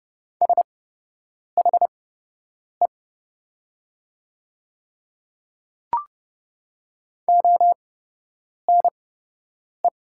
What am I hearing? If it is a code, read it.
Morse code, heard as a keyed tone, about 30 wpm, spaced as for 8 wpm, 700 Hz. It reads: SHE ONE
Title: 8x Character Spacing - Top 100 Words - 30wpm